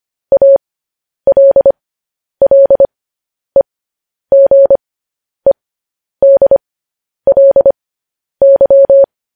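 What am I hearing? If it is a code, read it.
Morse code, heard as a keyed tone, about 25 wpm, spaced as for 10 wpm, 550 Hz. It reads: ALLEGEDLY
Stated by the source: Morse code news headlines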